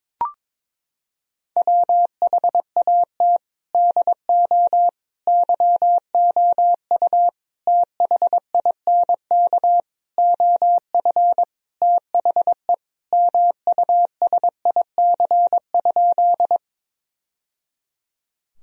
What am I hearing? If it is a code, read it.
Morse code, heard as a keyed tone, about 21 wpm, 700 Hz. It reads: WHAT DO YOU THINK OF THE MUSIC?
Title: Sentences from Top 1000 Words - Review Part 1 - 22wpm